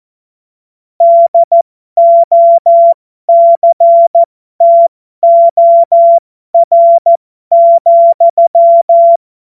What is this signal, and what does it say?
Morse code, heard as a keyed tone, about 14 wpm, spaced as for 12 wpm, 680 Hz. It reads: DOCTOR,
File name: Morsekurs / Witze